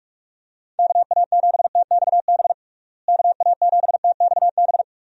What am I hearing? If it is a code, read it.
Morse code, heard as a keyed tone, about 46 wpm, 700 Hz. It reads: KA7TXB KA7TXB
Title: USA Call Signs - 6 Characters - Repeated 2x - 45wpm